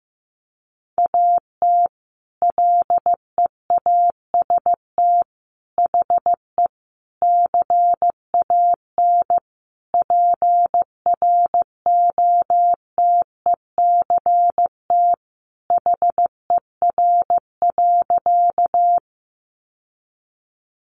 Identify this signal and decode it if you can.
Morse code, heard as a keyed tone, about 15 wpm, 700 Hz. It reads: AT LEAST HE CAN PROTECT HER.
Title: Sentences from Top 900 Words 15wpm